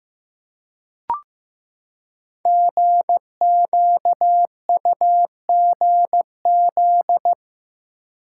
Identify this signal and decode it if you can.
Morse code, heard as a keyed tone, about 15 wpm, 700 Hz. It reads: GQUGZ
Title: Code Groups - 5 Letters - 15wpm